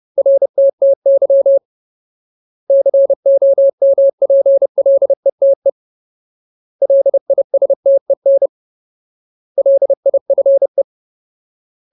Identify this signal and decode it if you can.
Morse code, heard as a keyed tone, about 30 wpm, 550 Hz. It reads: RTTY COMPLETE LISTEN LIFE
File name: Ultimate Word List / 30WPM Ultimate